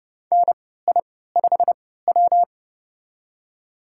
Morse code, heard as a keyed tone, about 30 wpm, spaced as for 17 wpm, 700 Hz. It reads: NI5W
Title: USA Call Signs - 4 Characters - 3x Character Spacing - 30wpm